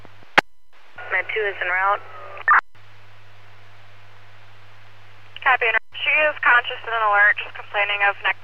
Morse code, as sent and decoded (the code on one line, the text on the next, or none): none